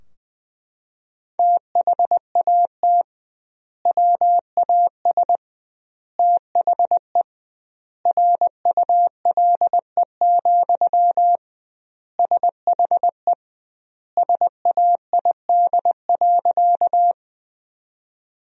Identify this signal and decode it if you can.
Morse code, heard as a keyed tone, about 21 wpm, 700 Hz. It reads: THAT WAS THE RULE, SHE SAID.
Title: 2x Word-Spacing - Sentences from Top 400 Words - 20wpm